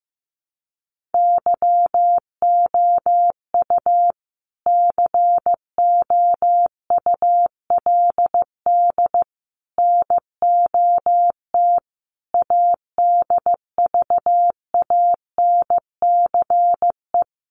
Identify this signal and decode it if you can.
Morse code, heard as a keyed tone, about 15 wpm, 700 Hz. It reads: YOU COULD NOT ADVANCE
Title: Sentences from Top 1400 Words 15wpm